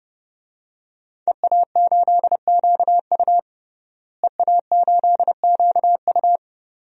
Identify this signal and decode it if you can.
Morse code, heard as a keyed tone, about 32 wpm, 700 Hz. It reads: EA8QU EA8QU
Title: DXCC-100 Call Signs - 5 Characters - Repeated 2x - 30wpm